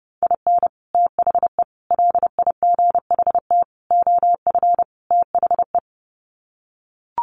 Morse code, heard as a keyed tone, about 28 wpm, 700 Hz. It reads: IN THE LIGHT OF THE
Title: Sets of 5 Words Encore2 30wpm